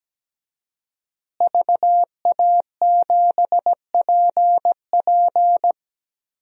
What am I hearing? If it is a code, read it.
Morse code, heard as a keyed tone, about 17 wpm, 700 Hz. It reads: VA7PP
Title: DXCC-100 Call Signs - 5 Characters - Repeated 3x - 17wpm